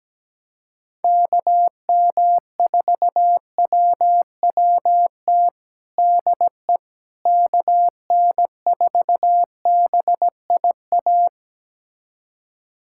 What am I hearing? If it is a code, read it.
Morse code, heard as a keyed tone, about 17 wpm, 700 Hz. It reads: KM4WWT DE KN4BIA
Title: USA Callsign DE Callsign 17wpm